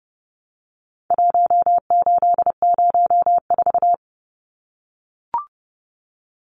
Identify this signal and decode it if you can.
Morse code, heard as a keyed tone, about 30 wpm, 700 Hz. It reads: 1804
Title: Numbers - Contest Style - 4-Digit 30wpm